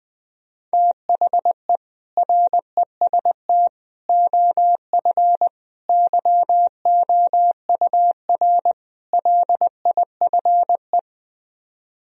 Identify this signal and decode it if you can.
Morse code, heard as a keyed tone, about 20 wpm, 700 Hz. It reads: THE REST OF YOUR LIFE